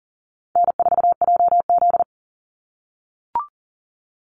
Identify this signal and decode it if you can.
Morse code, heard as a keyed tone, about 40 wpm, 700 Hz. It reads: N4JZ
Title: USA Call Signs 40wpm